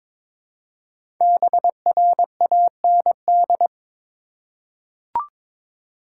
Morse code, heard as a keyed tone, about 23 wpm, 700 Hz. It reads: BRAND
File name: 5 Letter Words - First 3 Letters Given - 22wpm